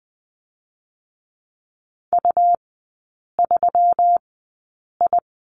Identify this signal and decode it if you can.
Morse code, heard as a keyed tone, about 20 wpm, spaced as for 8 wpm, 700 Hz. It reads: U3I
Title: 218 Lesson 35e - Number 0 - Three Characters 20wpm